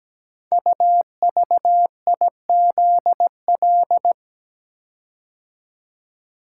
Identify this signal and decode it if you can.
Morse code, heard as a keyed tone, about 17 wpm, 700 Hz. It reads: UVIZL
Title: Code Groups - 5 Letters - 17wpm